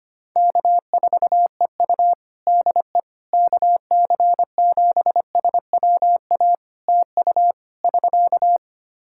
Morse code, heard as a keyed tone, about 25 wpm, 700 Hz. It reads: K4EU DE KC7SWA TU <SK>
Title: Practice QSO - Exchange 4 - Ending - 25wpm